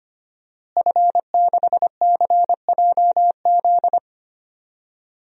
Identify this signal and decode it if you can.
Morse code, heard as a keyed tone, about 25 wpm, 700 Hz. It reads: F6CJZ
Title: DXCC-100 Call Signs - 5 Characters - Repeated 3x - 25wpm